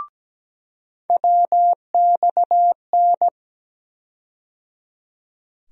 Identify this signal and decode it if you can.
Morse code, heard as a keyed tone, about 17 wpm, 700 Hz. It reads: WXN
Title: USA Call Sign Postfixes - 3 Letters 17wpm